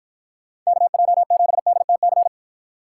Morse code, heard as a keyed tone, about 50 wpm, 700 Hz. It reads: KJ7DTC